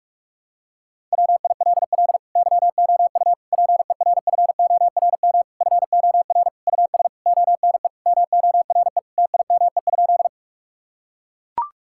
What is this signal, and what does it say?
Morse code, heard as a keyed tone, about 41 wpm, 700 Hz. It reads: WILL YOU PERFORM FOR US ONE MORE TIME?
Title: Sentences from Top 1000 Words - Review Part 5 - 45wpm